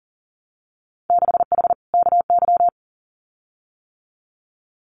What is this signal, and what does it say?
Morse code, heard as a keyed tone, about 38 wpm, 700 Hz. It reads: 6H KY
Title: Winter Field Day Exchanges Encore 40wpm